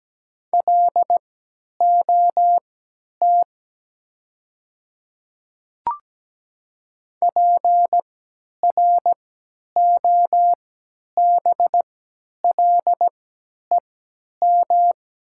Morse code, heard as a keyed tone, about 17 wpm, spaced as for 10 wpm, 700 Hz. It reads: LOT PROBLEM
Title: 3x Character Spacing - Top 500 Words - 17wpm